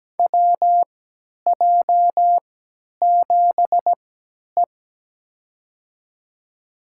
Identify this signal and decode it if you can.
Morse code, heard as a keyed tone, about 17 wpm, spaced as for 10 wpm, 700 Hz. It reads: WJ7E
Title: USA Call Signs - 4 Characters - 3x Character Spacing - 17wpm